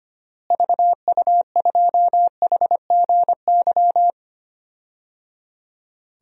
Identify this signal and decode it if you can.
Morse code, heard as a keyed tone, about 25 wpm, 700 Hz. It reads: VU2HGY
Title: DXCC-100 Call Signs - 6 Characters - Repeated 3x - 25wpm